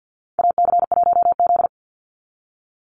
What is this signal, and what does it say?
Morse code, heard as a keyed tone, about 48 wpm, 700 Hz. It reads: AC1Z
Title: NA SOTA Activator Callsigns 50wpm